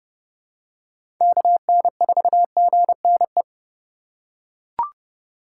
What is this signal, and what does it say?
Morse code, heard as a keyed tone, about 30 wpm, 700 Hz. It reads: KN4GNE
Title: USA Call Signs - 6 Characters - 2x Character Spacing - 30wpm